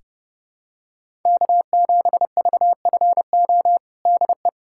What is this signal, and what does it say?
Morse code, heard as a keyed tone, about 30 wpm, 700 Hz. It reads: K7VFO DE
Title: USA Callsign DE Callsign - 5 Characters - 30wpm